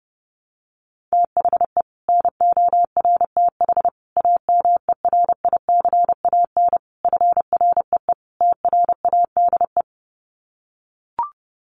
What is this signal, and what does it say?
Morse code, heard as a keyed tone, about 30 wpm, 700 Hz. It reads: THE NORTH AMERICAN FREE TRADE
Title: Sets of 5 Words Encore 30wpm